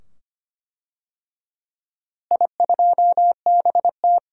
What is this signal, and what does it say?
Morse code, heard as a keyed tone, about 25 wpm, 700 Hz. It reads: I2BT
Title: DXCC-100 Call Signs - 4 Characters - Repeated 2x - 25wpm